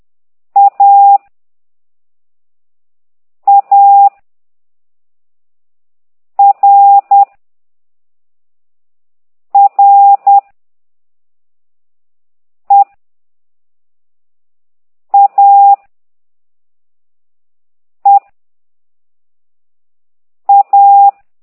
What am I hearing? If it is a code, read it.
Morse code, heard as a keyed tone, about 10 wpm, 800 Hz. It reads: A A R R E A E A
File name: ct2idw / exerc1